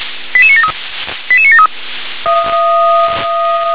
Five-tone selective call (ZVEI), heard as quick runs of five tones, tones of about 70 ms. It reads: 88973 88973
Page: ZVEI